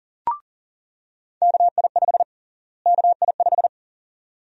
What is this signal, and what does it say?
Morse code, heard as a keyed tone, about 40 wpm, 700 Hz. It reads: KI5 KI5
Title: USA Call Sign Prefixes - Repeated 2x - 40wpm